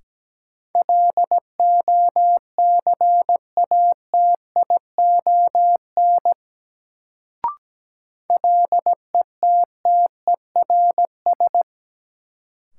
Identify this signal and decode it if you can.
Morse code, heard as a keyed tone, about 17 wpm, 700 Hz. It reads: LOCATION LETTERS
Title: Sentences from Top 1000 Words - Review Part 6 - New Word Drill - 17wpm